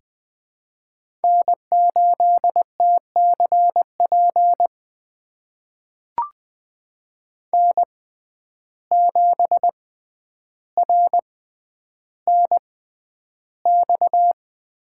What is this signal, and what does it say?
Morse code, heard as a keyed tone, about 20 wpm, spaced as for 10 wpm, 700 Hz. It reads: N8TCP N7RNX
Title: USA Call Signs - 5 Characters - 6x Character Spacing - 20wpm